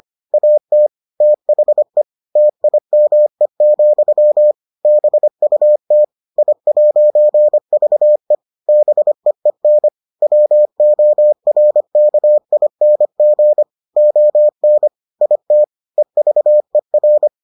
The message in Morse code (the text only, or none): AT THE TIME, BUT I'VE BEEN WORKING ON IT EVER